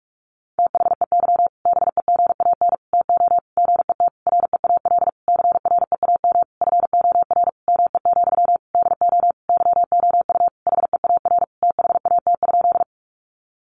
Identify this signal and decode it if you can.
Morse code, heard as a keyed tone, about 44 wpm, 700 Hz. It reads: THEY BEGAN TO GET REAL CREAM FOR ME, DO YOU HEAR THAT?
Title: Sentences from Top 1500 Words 45wpm